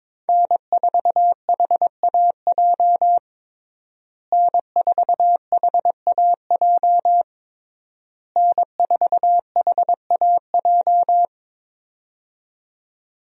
Morse code, heard as a keyed tone, about 22 wpm, 700 Hz. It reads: N4HAJ N4HAJ N4HAJ
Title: USA Call Signs - 5 Characters - Repeated 3x - 22wpm